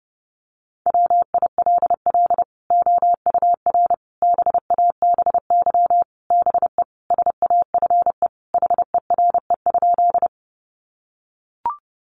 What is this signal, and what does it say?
Morse code, heard as a keyed tone, about 30 wpm, 700 Hz. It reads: WILL OUR BABY BE SAFE HERE?
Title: Sentences from Top 1300 Words 30wpm